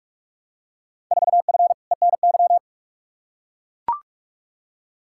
Morse code, heard as a keyed tone, about 41 wpm, 700 Hz. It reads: 4F ENY
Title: Field Day Exchanges - 45wpm